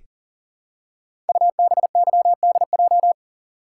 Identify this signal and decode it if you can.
Morse code, heard as a keyed tone, about 41 wpm, 690 Hz. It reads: UBYDJ
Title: Code Groups - 5 Letters - 40wpm